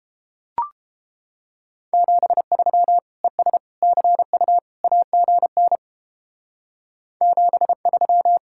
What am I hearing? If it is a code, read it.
Morse code, heard as a keyed tone, about 31 wpm, 700 Hz. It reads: 73 ES CU AGN 73